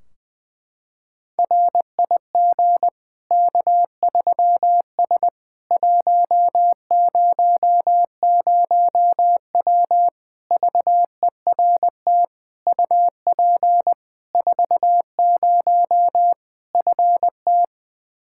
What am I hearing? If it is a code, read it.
Morse code, heard as a keyed tone, about 21 wpm, 700 Hz. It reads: RIG K3S 100W VERT UP 40 FT
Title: K1USN SST QSO Practice 20wpm